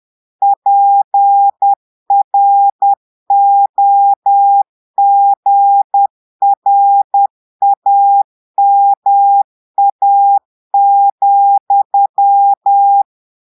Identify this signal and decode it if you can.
Morse code, heard as a keyed tone, about 10 wpm, 800 Hz. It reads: PROGRAMA,
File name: portugais texte en clair 2 vitesse 10 mots minutes